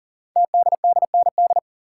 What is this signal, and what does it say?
Morse code, heard as a keyed tone, about 41 wpm, 700 Hz. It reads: TDDND
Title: Code Groups - 5 Letters - 40wpm